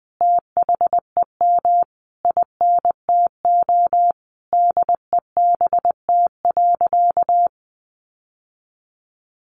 Morse code, heard as a keyed tone, about 20 wpm, 700 Hz. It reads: THEM INTO DEBT.
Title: Sentences from Top 1300 Words 20wpm